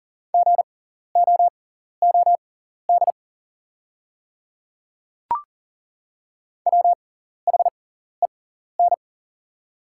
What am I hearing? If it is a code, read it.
Morse code, heard as a keyed tone, about 40 wpm, spaced as for 14 wpm, 690 Hz. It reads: GOOD WHEN
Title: 6x Character Spacing - Top 200 Words - 40wpm